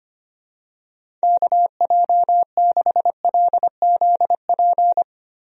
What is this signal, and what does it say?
Morse code, heard as a keyed tone, about 25 wpm, 700 Hz. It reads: KJ6LZP
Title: USA Call Signs - 6 Characters - Repeated 2x - 25wpm